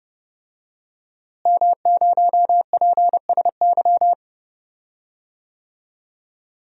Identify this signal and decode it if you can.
Morse code, heard as a keyed tone, about 30 wpm, 700 Hz. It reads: M0PSY
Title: DXCC-100 Call Signs - 5 Characters - Repeated 2x - 30wpm